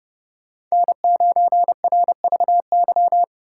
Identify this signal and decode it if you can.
Morse code, heard as a keyed tone, about 35 wpm, 700 Hz. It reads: N9RVY
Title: Speed Racing - USA 1x3 Call Signs 30wpm